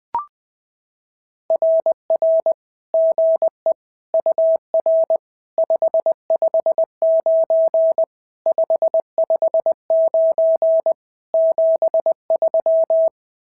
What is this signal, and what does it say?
Morse code, heard as a keyed tone, about 20 wpm, 650 Hz. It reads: RR GE UR 559 559 73
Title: SOTA QSO Practice 20wpm